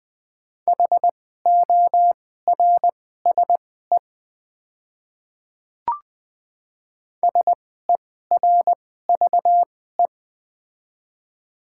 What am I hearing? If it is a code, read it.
Morse code, heard as a keyed tone, about 20 wpm, spaced as for 14 wpm, 690 Hz. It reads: HORSE SERVE
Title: 2x Character Spacing - Top 500 Words - 20wpm